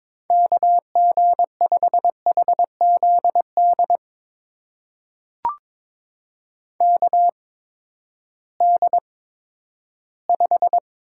Morse code, heard as a keyed tone, about 22 wpm, spaced as for 12 wpm, 700 Hz. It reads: KG5HZD KD5
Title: USA Call Signs - 6 Characters - 8x Character Spacing - 22wpm